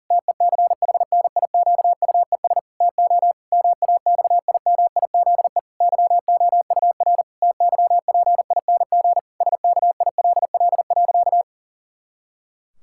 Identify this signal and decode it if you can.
Morse code, heard as a keyed tone, about 40 wpm, 700 Hz. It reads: TECHNIQUES TO MAXIMIZE YOUR TYPING SKILL.
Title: Pangram Challenge v2 - 40wpm